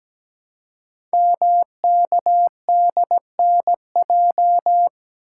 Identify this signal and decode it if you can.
Morse code, heard as a keyed tone, about 17 wpm, 700 Hz. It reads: MKDNJ
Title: Code Groups - 5 Letters - 17wpm